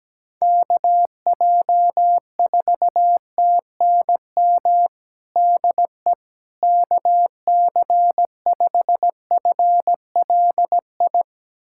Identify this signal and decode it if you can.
Morse code, heard as a keyed tone, about 17 wpm, 700 Hz. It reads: KJ4TNM DE KC5FLI
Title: USA Callsign DE Callsign - 6 Characters - 17wpm